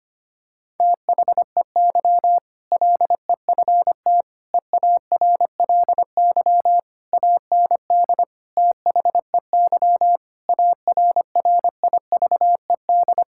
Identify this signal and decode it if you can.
Morse code, heard as a keyed tone, about 25 wpm, 700 Hz. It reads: THEY LEFT EARLY AND THEY ARRIVED
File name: practice / 25wpm / Sentences with 2-Word Phrases - 25wpm